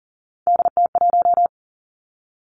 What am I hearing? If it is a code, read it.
Morse code, heard as a keyed tone, about 42 wpm, 690 Hz. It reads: DT1
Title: Three Challenging Letter-Number Combinations 40wpm